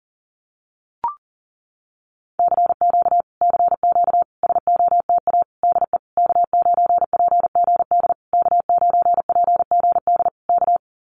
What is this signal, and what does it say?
Morse code, heard as a keyed tone, about 37 wpm, 700 Hz. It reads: CQ CQ SOTA DE K9PGD K9PGD K
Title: Common QSO Phrases Encore 40wpm